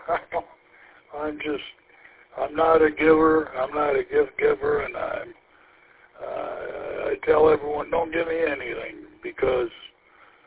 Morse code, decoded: none

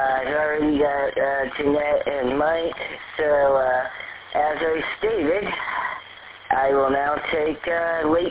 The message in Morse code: none

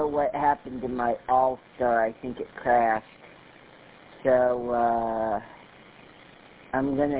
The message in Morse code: none

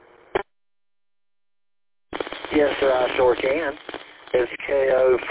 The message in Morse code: none

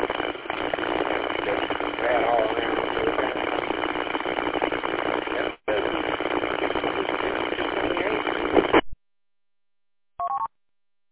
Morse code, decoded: none

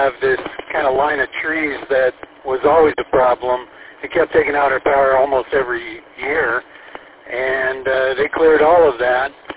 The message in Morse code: none

